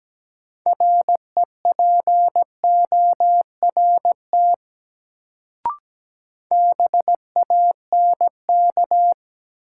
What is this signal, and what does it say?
Morse code, heard as a keyed tone, about 17 wpm, 700 Hz. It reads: REPORT BANK
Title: Sentences from Top 900 Words - New Word Drill - 17wpm